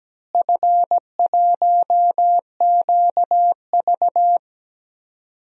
Morse code, heard as a keyed tone, about 17 wpm, 700 Hz. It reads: F1QV
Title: DXCC-100 Call Signs - 4 Characters - Repeated 2x - 17wpm